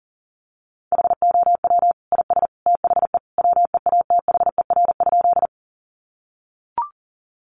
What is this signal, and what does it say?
Morse code, heard as a keyed tone, about 40 wpm, 700 Hz. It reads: HOW IS THE WEATHER?
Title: Sentences from Top 800 Words 40wpm